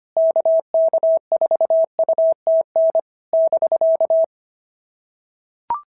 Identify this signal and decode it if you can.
Morse code, heard as a keyed tone, about 25 wpm, 650 Hz. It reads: KK4UTN <BK>
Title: SOTA QSO Practice 25wpm